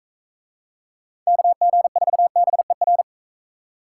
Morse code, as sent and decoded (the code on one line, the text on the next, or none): -.- --. ....- -... .-.
KG4BER